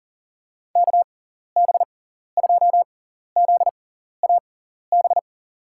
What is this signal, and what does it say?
Morse code, heard as a keyed tone, about 40 wpm, spaced as for 14 wpm, 700 Hz. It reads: KB2ZAB
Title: USA Call Signs - 6x Character Spacing - 40wpm